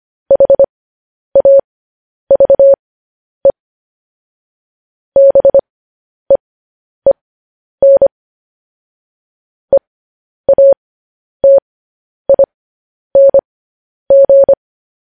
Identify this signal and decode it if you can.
Morse code, heard as a keyed tone, about 25 wpm, spaced as for 10 wpm, 550 Hz. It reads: HAVE BEEN EATING
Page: news headlines in morse code at 10 wpm